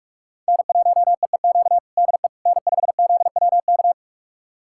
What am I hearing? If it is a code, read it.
Morse code, heard as a keyed tone, about 41 wpm, 700 Hz. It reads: N1EEQ DE N5ZWK
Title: USA Callsign DE Callsign - 5 Characters - 45wpm